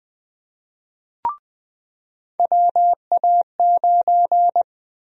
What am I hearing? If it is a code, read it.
Morse code, heard as a keyed tone, about 20 wpm, 700 Hz. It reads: WA9
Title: USA Call Sign Prefixes - 20wpm